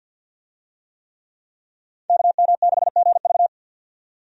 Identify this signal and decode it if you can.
Morse code, heard as a keyed tone, about 51 wpm, 700 Hz. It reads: KM6GV
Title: USA Call Signs - 5 Characters - 50wpm